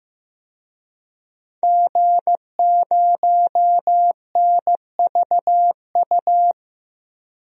Morse code, heard as a keyed tone, about 15 wpm, 700 Hz. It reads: G0NVU
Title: DXCC-100 Call Signs - 5 Characters - 15wpm